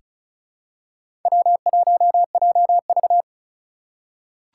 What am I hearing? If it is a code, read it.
Morse code, heard as a keyed tone, about 36 wpm, 700 Hz. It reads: W1JV
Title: USA Call Signs - 4 Characters - 3x Character Spacing - 35wpm